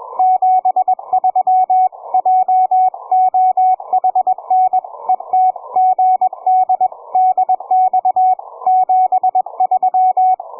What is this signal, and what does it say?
Morse code, heard as a keyed tone, about 22 wpm, 750 Hz, with radio noise. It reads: 73 JOHN ET GD DX 73
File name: cw forum / ve1zj 1830 160m 04h00